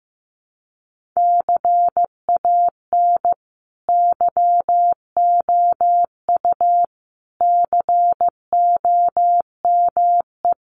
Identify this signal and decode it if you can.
Morse code, heard as a keyed tone, about 15 wpm, 700 Hz. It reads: CAN YOU COME
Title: Sentences from Top 1500 Words 15wpm